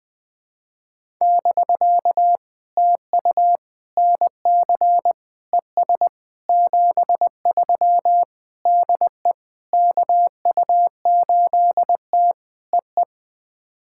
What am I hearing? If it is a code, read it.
Morse code, heard as a keyed tone, about 21 wpm, 700 Hz. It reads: <BK> TU NC ES 73 DE KU8T EE
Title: POTA QSO Practice - 20wpm